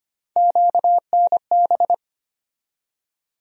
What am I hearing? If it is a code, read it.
Morse code, heard as a keyed tone, about 25 wpm, 700 Hz. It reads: QNB
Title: USA Call Sign Postfixes - 3 Letters 25wpm